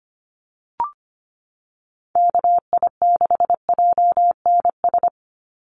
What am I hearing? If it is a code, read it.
Morse code, heard as a keyed tone, about 25 wpm, 700 Hz. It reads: KI6JNS
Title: USA Call Signs 25wpm